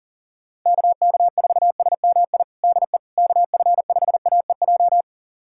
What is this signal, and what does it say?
Morse code, heard as a keyed tone, about 40 wpm, 700 Hz. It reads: KK4SMI DE KF5AEJ
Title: USA Callsign DE Callsign - 6 Characters - 40wpm